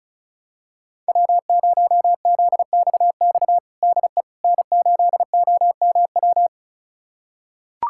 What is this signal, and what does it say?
Morse code, heard as a keyed tone, about 36 wpm, 700 Hz. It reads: W0ZXX DE N8OMW
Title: USA Callsign DE Callsign - 5 Characters - 35wpm